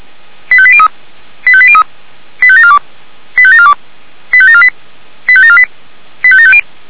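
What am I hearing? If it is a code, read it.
Five-tone selective call (ZVEI), heard as quick runs of five tones, tones of about 70 ms. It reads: 85803 85803 85742 85742 85748 85748 85750
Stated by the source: ZVEI five-tone selective call received on SDR.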